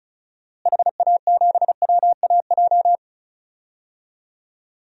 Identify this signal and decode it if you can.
Morse code, heard as a keyed tone, about 34 wpm, 700 Hz. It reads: HA7WAJ
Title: DXCC-100 Call Signs - 6 Characters - Repeated 2x - 35wpm